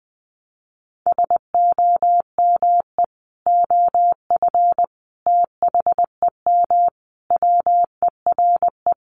Morse code, heard as a keyed tone, about 19 wpm, 700 Hz. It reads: SOME OF THEM WERE
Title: Sentences from Top 1200 Words 20wpm